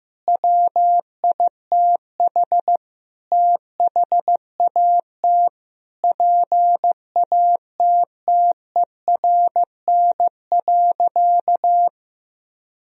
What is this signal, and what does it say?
Morse code, heard as a keyed tone, about 15 wpm, 700 Hz. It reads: WITH THAT PATTERN.
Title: Sentences from Top 1000 Words - Review Part 2 - 15wpm